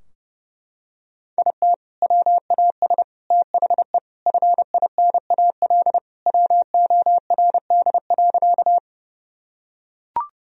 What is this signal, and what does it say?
Morse code, heard as a keyed tone, about 32 wpm, 700 Hz. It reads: IT WAS THE FINAL WORD.
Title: Sentences from Top 1000 Words - Review Part 2 - 30wpm